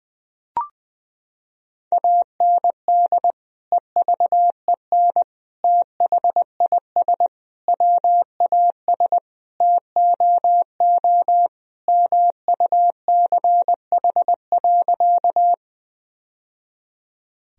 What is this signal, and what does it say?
Morse code, heard as a keyed tone, about 21 wpm, 700 Hz. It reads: AND EVEN THIS WAS TOO MUCH.
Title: Sentences from Top 1000 Words - Review Part 1 - 20wpm